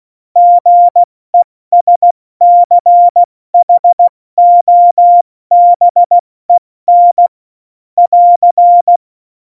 Morse code, heard as a keyed tone, about 16 wpm, spaced as for 14 wpm, 700 Hz. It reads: GESCHOBEN <AR>
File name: Morsekurs / Witze / Witz 30. April 2019, 70 F80